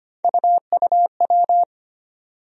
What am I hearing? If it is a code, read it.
Morse code, heard as a keyed tone, about 25 wpm, 700 Hz. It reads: UUW
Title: USA Call Sign Postfixes - 3 Letters 25wpm